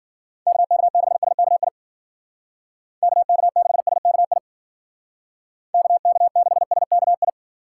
Morse code, heard as a keyed tone, about 51 wpm, 700 Hz. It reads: KK6SKI KK6SKI KK6SKI